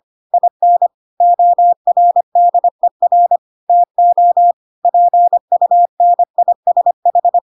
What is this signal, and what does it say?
Morse code, heard as a keyed tone, about 25 wpm, 700 Hz. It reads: IN ORDER TO PUNISH
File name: Pinocchio / Pinocchio ch8 25 25 ik1hge